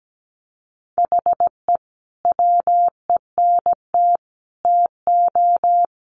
Morse code, heard as a keyed tone, about 17 wpm, 700 Hz. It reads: HE WENT TO
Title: Sentences from Top 200 Words 17wpm